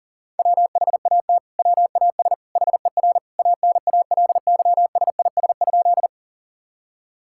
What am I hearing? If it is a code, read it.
Morse code, heard as a keyed tone, about 40 wpm, 700 Hz. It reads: WHAT WAS HER ANALYSIS?
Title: Sentences from Top 1000 Words - Review Part 5 - 40wpm